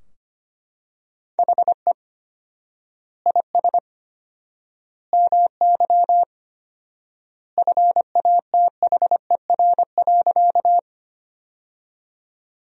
Morse code, heard as a keyed tone, about 25 wpm, 700 Hz. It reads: HE IS MY FATHER.
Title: 4x Word-Spacing - Sentences from Top 300 Words - 25wpm